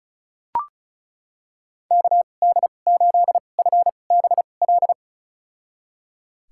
Morse code, heard as a keyed tone, about 36 wpm, spaced as for 25 wpm, 700 Hz. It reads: KD8FBL
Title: USA Call Signs - 6 Characters - 2x Character Spacing - 35wpm